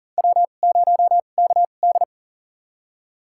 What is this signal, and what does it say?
Morse code, heard as a keyed tone, about 40 wpm, spaced as for 29 wpm, 700 Hz. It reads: W0KD
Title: USA Call Signs - 4 Characters - 2x Character Spacing - 40wpm